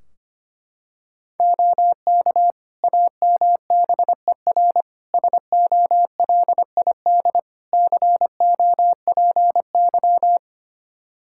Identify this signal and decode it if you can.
Morse code, heard as a keyed tone, about 25 wpm, 700 Hz. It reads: OK AMBER SOLID COPY
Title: Practice QSO - Exchange 3 - More Info - 25wpm